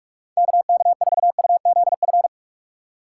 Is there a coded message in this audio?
Yes: Morse code from 0.4 to 2.3 s, heard as a keyed tone, about 44 wpm, 700 Hz. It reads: KK4UZF